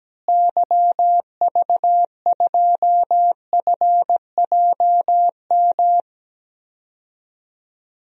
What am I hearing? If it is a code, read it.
Morse code, heard as a keyed tone, about 17 wpm, 700 Hz. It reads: YV2FJM
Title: DXCC-100 Call Signs - 6 Characters - Repeated 2x - 17wpm